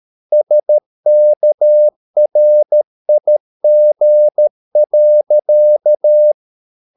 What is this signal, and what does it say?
Morse code, heard as a keyed tone, about 13 wpm, 600 Hz. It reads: SKRIG.
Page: Ferie 13 wpm